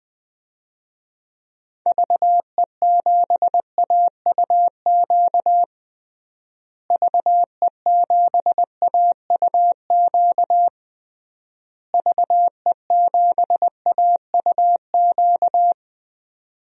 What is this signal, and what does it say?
Morse code, heard as a keyed tone, about 20 wpm, 700 Hz. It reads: VE7AUQ VE7AUQ VE7AUQ
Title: DXCC-100 Call Signs - 6 Characters - Repeated 3x - 20wpm